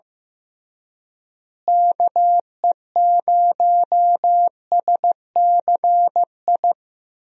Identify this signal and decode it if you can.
Morse code, heard as a keyed tone, about 15 wpm, 700 Hz. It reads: KE0SCI
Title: USA Call Signs - 6 Characters - Repeated 2x - 15wpm